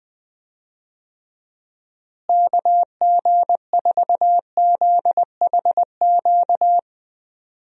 Morse code, heard as a keyed tone, about 21 wpm, 700 Hz. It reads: KG4ZHQ
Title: USA Call Signs - 6 Characters - Repeated 3x - 20wpm